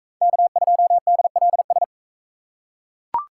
Morse code, heard as a keyed tone, about 44 wpm, 700 Hz. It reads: K2DLS